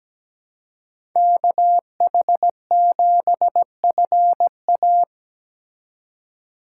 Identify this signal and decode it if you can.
Morse code, heard as a keyed tone, about 17 wpm, 700 Hz. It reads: KH7FA
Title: USA Call Signs - 5 Characters - 4x Character Spacing - 17wpm